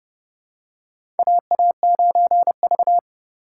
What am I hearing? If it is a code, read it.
Morse code, heard as a keyed tone, about 30 wpm, 700 Hz. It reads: AA9V
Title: USA Call Signs - 4 Characters - 4x Character Spacing - 30wpm